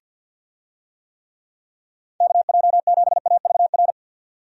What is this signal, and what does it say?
Morse code, heard as a keyed tone, about 50 wpm, 700 Hz. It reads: KJ7AVR
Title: USA Call Signs - 6 Characters - 50wpm